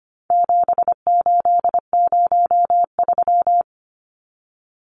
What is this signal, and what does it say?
Morse code, heard as a keyed tone, about 25 wpm, 700 Hz. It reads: 7803
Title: Numbers - 4-Digit 25wpm